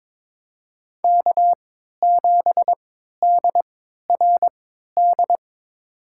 Morse code, heard as a keyed tone, about 22 wpm, spaced as for 13 wpm, 700 Hz. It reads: K7DRD
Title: USA Call Signs - 5 Characters - 3x Character Spacing - 22wpm